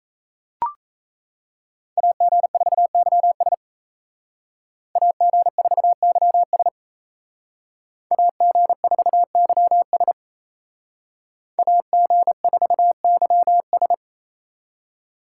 Morse code, heard as a keyed tone, about 37 wpm, 700 Hz. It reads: AG4YS AG4YS AG4YS AG4YS